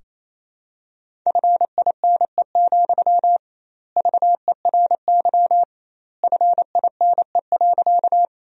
Morse code, heard as a keyed tone, about 28 wpm, 700 Hz. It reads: FINE, VERY FINE.
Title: 2x Word-Spacing - Sentences from Top 500 Words - 28wpm